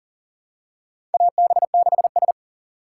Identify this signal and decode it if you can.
Morse code, heard as a keyed tone, about 42 wpm, 700 Hz. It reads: AB6S
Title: USA Call Signs - 4 Characters - 3x Character Spacing - 40wpm